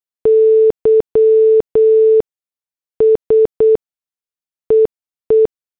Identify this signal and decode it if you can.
Morse code, heard as a keyed tone, about 8 wpm, spaced as for 7 wpm, 440 Hz. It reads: Y SEE
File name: morse / headlines